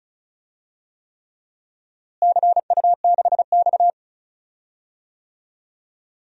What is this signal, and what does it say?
Morse code, heard as a keyed tone, about 34 wpm, 700 Hz. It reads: CU6X
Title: DXCC-100 Call Signs - 4 Characters - Repeated 3x - 35wpm